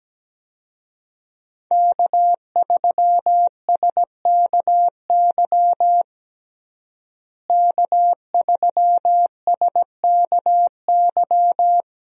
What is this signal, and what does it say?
Morse code, heard as a keyed tone, about 17 wpm, 700 Hz. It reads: K3SKY K3SKY
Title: USA Call Signs - 5 Characters - Repeated 2x - 17wpm